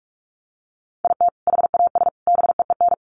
Morse code, heard as a keyed tone, about 44 wpm, 700 Hz. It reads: IT HAS BEEN
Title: Sets of 3 Words 45wpm